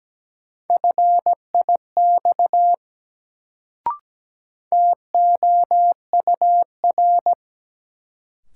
Morse code, heard as a keyed tone, about 17 wpm, 700 Hz. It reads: FIX TOUR
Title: Sentences from Top 1400 Words - New Word Drill - 17wpm